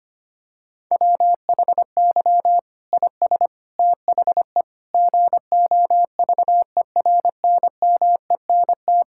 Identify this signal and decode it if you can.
Morse code, heard as a keyed tone, about 25 wpm, 700 Hz. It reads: WHY IS THE GOVERNMENT
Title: Sentences from Top 1000 Words - Review Part 5 - 25wpm